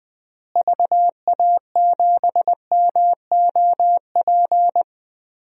Morse code, heard as a keyed tone, about 20 wpm, 700 Hz. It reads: VA7MOP
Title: DXCC-100 Call Signs - 6 Characters - Repeated 3x - 20wpm